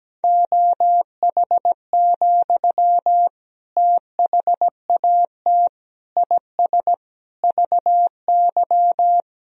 Morse code, heard as a keyed tone, about 17 wpm, 700 Hz. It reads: OH, THAT IS VY
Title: Sentences from Top 500 Words and Abbreviations 17wpm